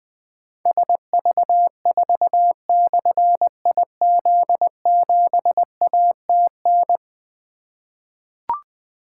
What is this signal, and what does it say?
Morse code, heard as a keyed tone, about 20 wpm, 700 Hz. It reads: SV4/IZ7ATN
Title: Difficult Call Signs 20wpm